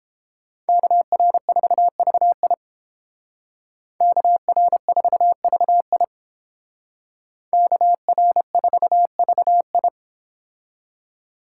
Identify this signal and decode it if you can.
Morse code, heard as a keyed tone, about 30 wpm, 700 Hz. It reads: KR4VI KR4VI KR4VI